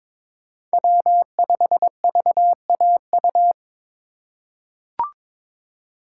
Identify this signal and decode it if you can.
Morse code, heard as a keyed tone, about 22 wpm, 700 Hz. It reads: W5VAU
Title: USA Call Signs - 5 Characters - 3x Character Spacing - 22wpm